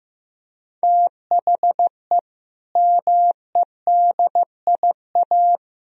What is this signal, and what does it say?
Morse code, heard as a keyed tone, about 15 wpm, 700 Hz. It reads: THE MEDIA
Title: Sentences from Top 1000 Words - Review Part 4 - 15wpm